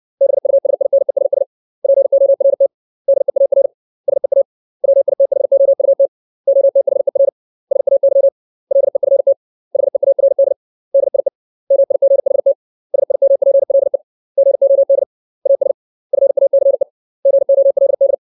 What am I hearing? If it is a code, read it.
Morse code, heard as a keyed tone, about 60 wpm, 550 Hz. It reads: BUSINESS PORT DEAR SEA WITHOUT OTHER SAY LEFT HEARD DIE MIGHT SIMPLE GOD AS FACE GOLD
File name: 632 words / 632 Words 60WPM